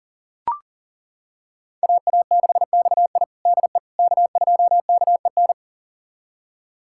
Morse code, heard as a keyed tone, about 40 wpm, 700 Hz. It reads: AA6XI DE K2KEN
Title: USA Callsign DE Callsign - 5 Characters - 40wpm